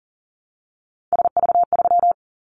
Morse code, heard as a keyed tone, about 42 wpm, 700 Hz. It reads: SV3
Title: Callsign Prefixes for Top 50 DXCC - 40wpm